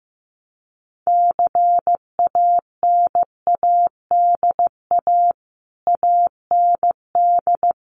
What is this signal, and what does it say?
Morse code, heard as a keyed tone, about 15 wpm, 700 Hz. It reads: CANADA AND
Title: Sets of 5 Words Encore 15wpm